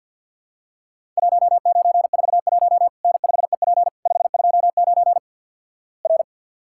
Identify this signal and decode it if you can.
Morse code, heard as a keyed tone, about 52 wpm, 700 Hz. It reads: 1941 N5EP 529 R
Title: Worked All Europe DX Contest - QTC Exchange with Repeats - 50wpm